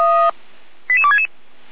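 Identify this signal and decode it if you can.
Five-tone selective call (ZVEI), heard as quick runs of five tones, tones of about 70 ms. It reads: 88266
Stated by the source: ZVEI five-tone selective call 88314 received on SDR.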